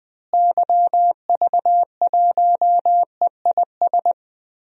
Morse code, heard as a keyed tone, about 20 wpm, 700 Hz. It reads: YV1EIS
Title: DXCC-100 Call Signs - 6 Characters - Repeated 3x - 20wpm